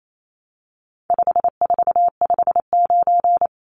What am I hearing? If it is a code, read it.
Morse code, heard as a keyed tone, about 28 wpm, 700 Hz. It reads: 5459
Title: Numbers - 4-Digit 28wpm